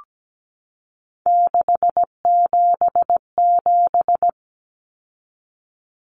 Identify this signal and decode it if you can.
Morse code, heard as a keyed tone, about 17 wpm, 700 Hz. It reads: 677
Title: Numbers - 3-Digit 17wpm